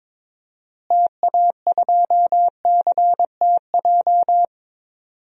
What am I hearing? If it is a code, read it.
Morse code, heard as a keyed tone, about 22 wpm, 700 Hz. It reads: TA2CTJ
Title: DXCC-100 Call Signs - 6 Characters - Repeated 2x - 22wpm